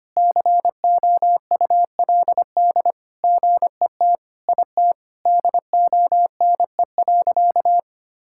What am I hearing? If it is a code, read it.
Morse code, heard as a keyed tone, about 25 wpm, 700 Hz. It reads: COULD GET IT DONE.
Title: Sentences from Top 1000 Words - Review Part 5 - 25wpm